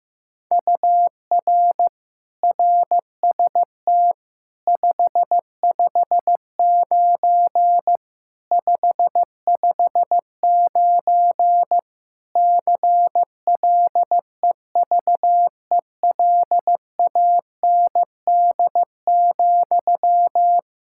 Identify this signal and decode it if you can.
Morse code, heard as a keyed tone, about 15 wpm, 700 Hz. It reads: UR RST 559 559 CLEVELAND,
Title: Practice QSO - Exchange 1 - Beginning - 15wpm